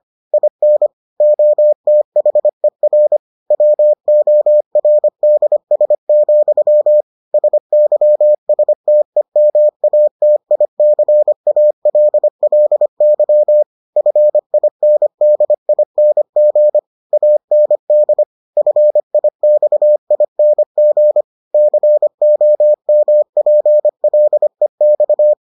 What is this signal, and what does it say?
Morse code, heard as a keyed tone, about 25 wpm, 600 Hz. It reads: IN OTHER WORDS, SYSTEMATICALLY FINDING AND FIXING COMPLEX